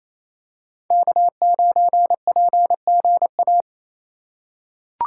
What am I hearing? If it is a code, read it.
Morse code, heard as a keyed tone, about 28 wpm, 700 Hz. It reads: K9PGA
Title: USA Call Signs - 5 Characters - 2x Character Spacing - 28wpm